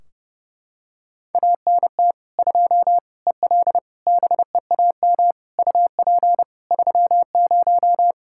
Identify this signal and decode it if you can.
Morse code, heard as a keyed tone, about 32 wpm, 700 Hz. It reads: ANT 2 EL BEAM UP 30
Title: Practice QSO - Exchange 2 - Info - 30wpm